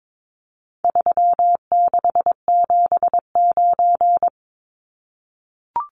Morse code, heard as a keyed tone, about 22 wpm, 700 Hz. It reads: 3679
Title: Numbers - 4-Digit 22wpm